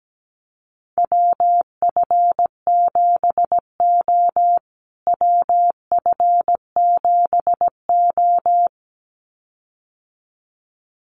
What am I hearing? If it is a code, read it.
Morse code, heard as a keyed tone, about 17 wpm, 700 Hz. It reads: WF7O WF7O
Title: USA Call Signs Repeated 2x 17wpm